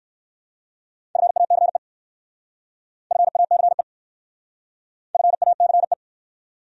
Morse code, heard as a keyed tone, about 58 wpm, 700 Hz. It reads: FACE FACE FACE